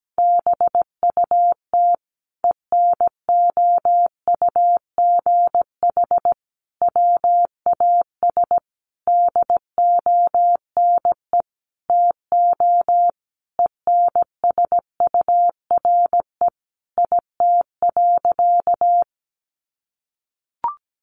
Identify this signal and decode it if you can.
Morse code, heard as a keyed tone, about 17 wpm, 700 Hz. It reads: BUT ENOUGH WAS DONE TO ENSURE IT.
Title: Sentences from Top 1100 Words 17wpm